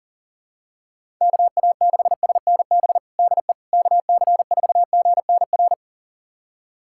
Morse code, heard as a keyed tone, about 40 wpm, 700 Hz. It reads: KA6SNB DE KC4GNR
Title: USA Callsign DE Callsign - 6 Characters - 40wpm